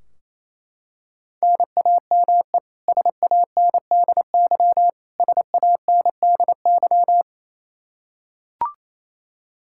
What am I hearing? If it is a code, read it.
Morse code, heard as a keyed tone, about 28 wpm, 700 Hz. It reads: NAME SANDY SANDY
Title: K1USN SST QSO Practice 28wpm